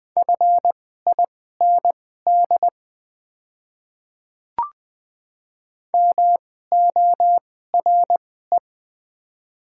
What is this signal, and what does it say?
Morse code, heard as a keyed tone, about 20 wpm, spaced as for 14 wpm, 700 Hz. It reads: FIND MORE
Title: 2x Character Spacing - Top 100 Words - 20wpm